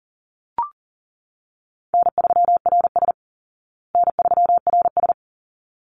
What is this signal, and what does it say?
Morse code, heard as a keyed tone, about 40 wpm, 700 Hz. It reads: N3RS N3RS
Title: Easy Call Signs Repeated 2x - 40wpm